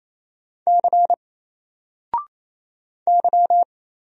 Morse code, heard as a keyed tone, about 28 wpm, 700 Hz. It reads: C Y
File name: Single-Letter-Number-28wpm